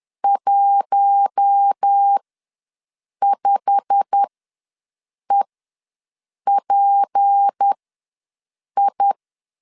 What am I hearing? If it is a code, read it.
Morse code, heard as a keyed tone, about 10 wpm, spaced as for 6 wpm, 800 Hz. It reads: 15EPI
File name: LEZIONI / 8 YJ45 / 8 YJ45 ws 10wpm 3843